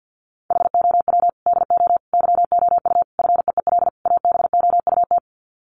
Morse code, heard as a keyed tone, about 50 wpm, 700 Hz. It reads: HOW DO YOU FEEL ABOUT
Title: Sets of 5 Words 50wpm